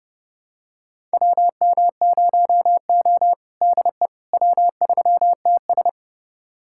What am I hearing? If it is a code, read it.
Morse code, heard as a keyed tone, about 30 wpm, 700 Hz. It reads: WM0O DE W3TS
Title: USA Callsign DE Callsign - 4 Characters - 30wpm